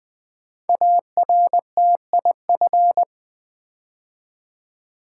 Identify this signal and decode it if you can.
Morse code, heard as a keyed tone, about 20 wpm, 700 Hz. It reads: ARTIF
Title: Build A Word - 5 Letters At A Time - 10-Letter Words - 20wpm